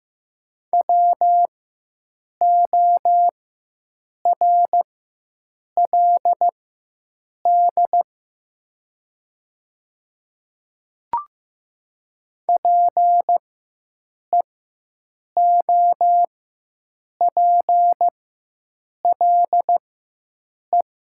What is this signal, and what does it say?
Morse code, heard as a keyed tone, about 15 wpm, spaced as for 7 wpm, 700 Hz. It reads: WORLD PEOPLE
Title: 4x Character Spacing - Top 200 Words - 15wpm